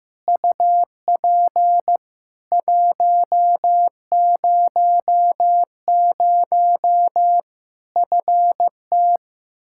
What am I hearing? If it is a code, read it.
Morse code, heard as a keyed tone, about 15 wpm, 700 Hz. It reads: UP 100 FT
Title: Practice QSO - Exchange 2 - Info - 15wpm